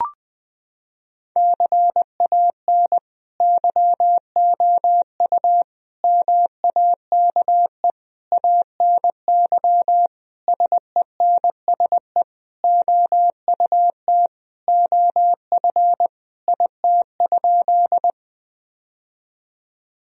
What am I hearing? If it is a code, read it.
Morse code, heard as a keyed tone, about 21 wpm, 700 Hz. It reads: CAN YOU MAKE ANY SENSE OUT OF IT?
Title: Sentences from Top 1000 Words - Review Part 4 - 20wpm